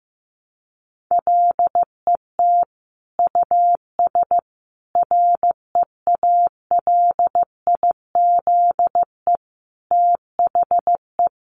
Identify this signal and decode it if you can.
Morse code, heard as a keyed tone, about 15 wpm, 700 Hz. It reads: LET US REALIZE THE
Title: Sentences from Top 1300 Words 15wpm